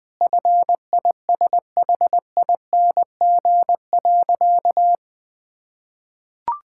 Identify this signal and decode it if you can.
Morse code, heard as a keyed tone, about 20 wpm, 700 Hz. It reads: FISHING.
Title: Sentences from Top 1000 Words - Review Part 5 - 20wpm